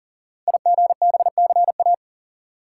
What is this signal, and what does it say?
Morse code, heard as a keyed tone, about 38 wpm, 690 Hz. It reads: IGBCA
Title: Code Groups - 5 Letters - 40wpm